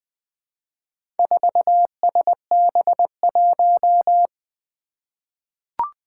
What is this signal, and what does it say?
Morse code, heard as a keyed tone, about 21 wpm, 700 Hz. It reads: 4SB1
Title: Four Challenging Letter-Number Combinations 20wpm